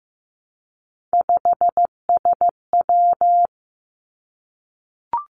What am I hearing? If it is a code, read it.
Morse code, heard as a keyed tone, about 15 wpm, 700 Hz. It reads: 5SW
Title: Three Challenging Letter-Number Combinations 15wpm